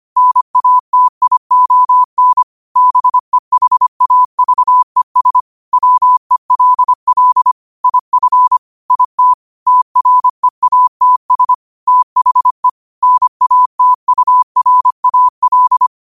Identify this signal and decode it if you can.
Morse code, heard as a keyed tone, about 25 wpm, 1000 Hz. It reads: NATION BEHAVES WELL IF IT TREATS THE NATURAL